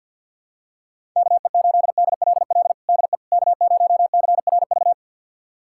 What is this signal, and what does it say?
Morse code, heard as a keyed tone, about 46 wpm, 700 Hz. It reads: KE8DLL DE K0CRV
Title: USA Callsign DE Callsign 50wpm